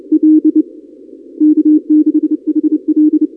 Morse code, heard as a keyed tone, about 27 wpm, 310 Hz. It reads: L K6HL
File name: contests / 2019-cw-wpx-kt5j-k5pi- -2330 0-31